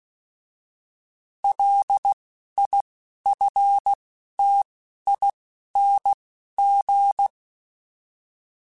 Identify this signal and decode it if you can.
Morse code, heard as a keyed tone, about 16 wpm, 800 Hz. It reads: L I F T I N G
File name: Alphabetical / Alphabetical mp3s / Ls @10